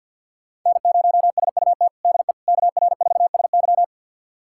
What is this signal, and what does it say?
Morse code, heard as a keyed tone, about 52 wpm, 700 Hz. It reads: N0SUT DE KR4SY
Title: USA Callsign DE Callsign - 5 Characters - 50wpm